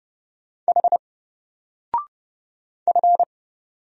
Morse code, heard as a keyed tone, about 31 wpm, 700 Hz. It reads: H F
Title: Single Letter-Number 30wpm